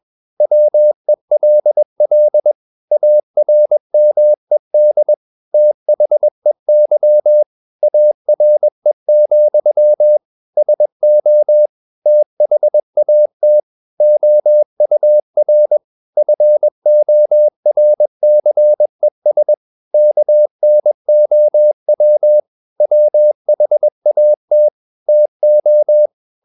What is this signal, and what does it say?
Morse code, heard as a keyed tone, about 21 wpm, 600 Hz. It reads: WELL ARMED THEY ARE, SO THAT OUR FORCES KNOW WHAT TO
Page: CW med Gnister